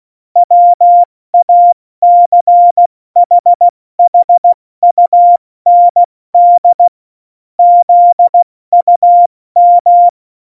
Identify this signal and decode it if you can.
Morse code, heard as a keyed tone, about 16 wpm, 700 Hz. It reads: WACHHUND ZUM
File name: Morsekurs / Witze / Witz 12. Juni 2018, 70 F80